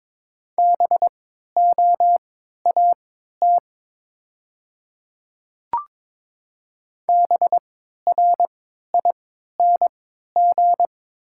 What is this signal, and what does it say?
Morse code, heard as a keyed tone, about 22 wpm, spaced as for 13 wpm, 700 Hz. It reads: BOAT BRING
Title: 3x Character Spacing - Top 500 Words - 22wpm